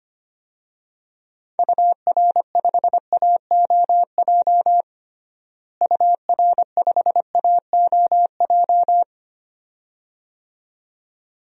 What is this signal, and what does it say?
Morse code, heard as a keyed tone, about 25 wpm, 700 Hz. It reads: UR5AOJ UR5AOJ
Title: DXCC-100 Call Signs - 6 Characters - Repeated 2x - 25wpm